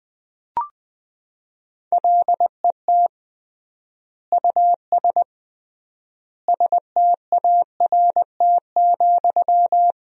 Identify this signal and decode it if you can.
Morse code, heard as a keyed tone, about 20 wpm, 700 Hz. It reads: LET US START,
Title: 3x Word-Spacing - Sentences from Top 300 Words - 20wpm